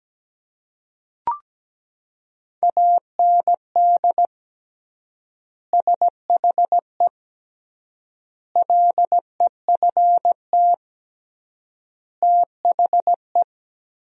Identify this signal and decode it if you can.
Morse code, heard as a keyed tone, about 17 wpm, 700 Hz. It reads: AND SHE LEFT THE
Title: 3x Word-Spacing - Sentences from Top 300 Words - 17wpm